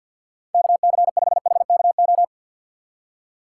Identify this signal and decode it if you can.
Morse code, heard as a keyed tone, about 53 wpm, 700 Hz. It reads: KC5HKO
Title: USA Call Signs - 6 Characters - 4x Character Spacing - 50wpm